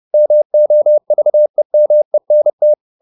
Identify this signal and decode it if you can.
Morse code, heard as a keyed tone, about 30 wpm, 600 Hz. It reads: MOVEMENT
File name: the old man and the sea / the old man and the sea 1 30wpm